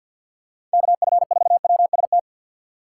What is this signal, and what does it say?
Morse code, heard as a keyed tone, about 53 wpm, 700 Hz. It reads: KF4WST
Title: USA Call Signs - 3x Character Spacing - 50wpm